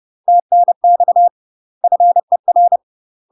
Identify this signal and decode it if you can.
Morse code, heard as a keyed tone, about 30 wpm, 700 Hz. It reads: TNX FER
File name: UK MASTER QSOs TRAINING / 30WPM / UK MASTER QSOs 0002